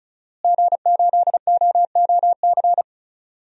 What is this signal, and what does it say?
Morse code, heard as a keyed tone, about 36 wpm, 700 Hz. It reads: G8OOC
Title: DXCC-100 Call Signs - 5 Characters - Repeated 3x - 35wpm